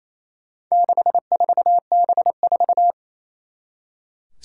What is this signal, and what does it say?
Morse code, heard as a keyed tone, about 28 wpm, 700 Hz. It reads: 64B4
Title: Four Challenging Letter-Number Combinations 28wpm